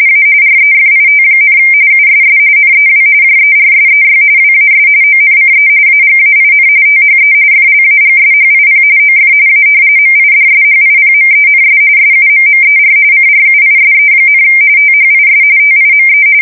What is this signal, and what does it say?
RTTY, about 45.45 baud. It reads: A REFIT, USS NEW JERSEY WAS ONCE AGAIN THE FLAGSHIP OF A MIDSHIPMENS SUMMER CRUISE, THIS TIME
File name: fsk170 / battleshipnj